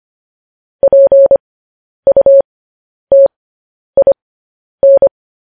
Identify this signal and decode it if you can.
Morse code, heard as a keyed tone, about 25 wpm, spaced as for 10 wpm, 550 Hz. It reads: PUTIN
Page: news-headlines-in-morse-code-at-10-wpm